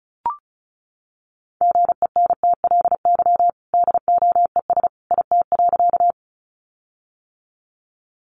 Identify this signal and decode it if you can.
Morse code, heard as a keyed tone, about 34 wpm, 700 Hz. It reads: GENTLY DOES IT.
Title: Sentences from Top 1500 Words 35wpm